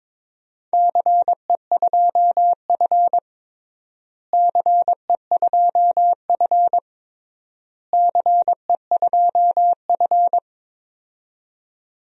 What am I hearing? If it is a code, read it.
Morse code, heard as a keyed tone, about 21 wpm, 700 Hz. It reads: CE2F CE2F CE2F
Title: DXCC-100 Call Signs - 4 Characters - Repeated 3x - 22wpm